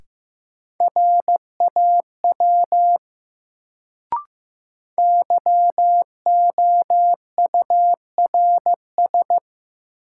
Morse code, heard as a keyed tone, about 15 wpm, 700 Hz. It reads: RAW YOURS
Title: Sentences from Top 1500 Words - New Word Drill Alt - 15wpm